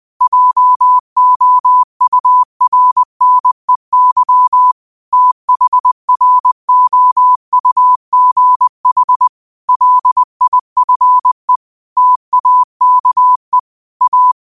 Morse code, heard as a keyed tone, about 20 wpm, 1000 Hz. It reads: JOURNEY THROUGH LIFE TAKE A
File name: qotdfiles / 20wpm